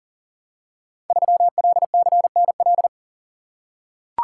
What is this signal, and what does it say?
Morse code, heard as a keyed tone, about 38 wpm, 700 Hz. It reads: 3LCNL
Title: Code Groups - 5 Letter:Numbers - 40wpm